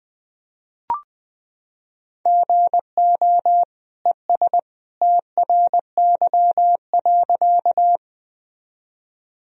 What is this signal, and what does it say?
Morse code, heard as a keyed tone, about 21 wpm, 700 Hz. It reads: GO ES TRY.
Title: Sentences from Top 200 Words and Abbreviations 20wpm